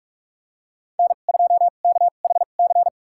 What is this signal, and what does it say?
Morse code, heard as a keyed tone, about 46 wpm, spaced as for 33 wpm, 700 Hz. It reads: N2KHC
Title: USA Call Signs - 5 Characters - 2x Character Spacing - 45wpm